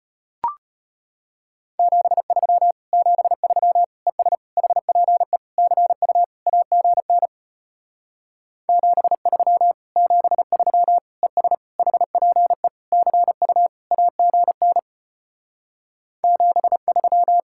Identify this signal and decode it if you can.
Morse code, heard as a keyed tone, about 33 wpm, 700 Hz. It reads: 73 73 ES HPE CU AGN 73 73 ES HPE CU AGN 73